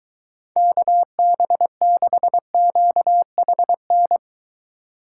Morse code, heard as a keyed tone, about 22 wpm, 700 Hz. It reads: KB6QHN